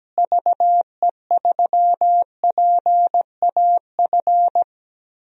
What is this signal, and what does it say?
Morse code, heard as a keyed tone, about 17 wpm, 700 Hz. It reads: VE3PAF
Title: DXCC-100 Call Signs - 6 Characters - Repeated 3x - 17wpm